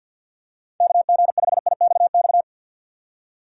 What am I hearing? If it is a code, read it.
Morse code, heard as a keyed tone, about 53 wpm, 700 Hz. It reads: KG5IXX
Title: USA Call Signs - 6 Characters - 4x Character Spacing - 50wpm